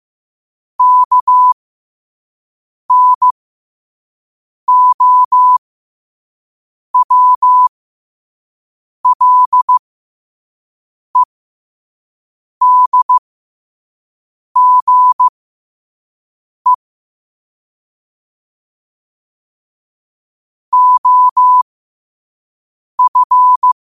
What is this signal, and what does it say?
Morse code, heard as a keyed tone, about 15 wpm, spaced as for 5 wpm, 1000 Hz. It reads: KNOWLEDGE OF